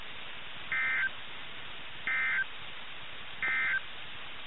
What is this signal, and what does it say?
Emergency Alert System (SAME) tones, heard as harsh buzzing bursts. It is heard from 0.6 to 3.9 s.
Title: NOAA Weather Radio - KIH61